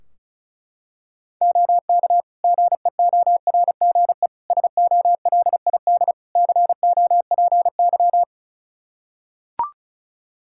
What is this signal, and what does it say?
Morse code, heard as a keyed tone, about 34 wpm, 700 Hz. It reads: OK GEORGE SOLID COPY
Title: K1USN SST QSO Practice 35wpm